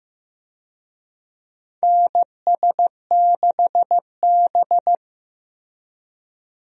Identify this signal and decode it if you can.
Morse code, heard as a keyed tone, about 15 wpm, 700 Hz. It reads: NS6B
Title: USA Call Signs - 4 Characters - 15wpm